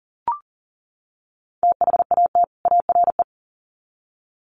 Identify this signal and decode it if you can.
Morse code, heard as a keyed tone, about 40 wpm, 700 Hz. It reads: THAT ARE
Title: Sets of 2 Words 40wpm